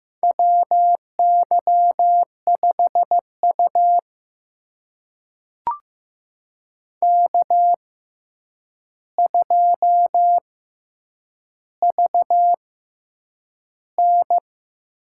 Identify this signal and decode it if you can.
Morse code, heard as a keyed tone, about 15 wpm, spaced as for 8 wpm, 700 Hz. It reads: WY5U K2VN
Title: USA Call Signs - 4 Characters - 6x Character Spacing - 15wpm